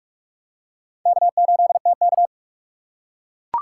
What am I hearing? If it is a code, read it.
Morse code, heard as a keyed tone, about 43 wpm, 700 Hz. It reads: K8TK